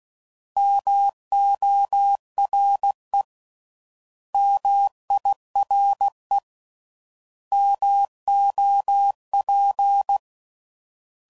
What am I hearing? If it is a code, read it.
Morse code, heard as a keyed tone, about 16 wpm, 800 Hz. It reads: MORE MIRE MOP